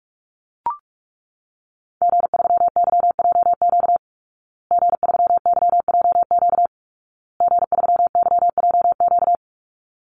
Morse code, heard as a keyed tone, about 44 wpm, 700 Hz. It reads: G3YJQ G3YJQ G3YJQ
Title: Call Signs Repeated 2x - 45wpm